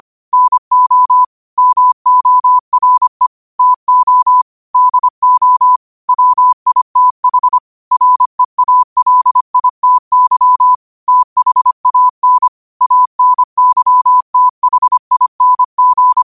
Morse code, heard as a keyed tone, about 25 wpm, 1000 Hz. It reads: NO MORE TO DO WITH REALITY THAN ANYTHING
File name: qotdfiles / 25wpm